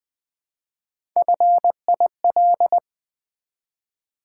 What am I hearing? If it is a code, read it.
Morse code, heard as a keyed tone, about 20 wpm, 700 Hz. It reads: FIL
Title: Build A Word - 1 Letter At A Time - 4-Letter Words - 20wpm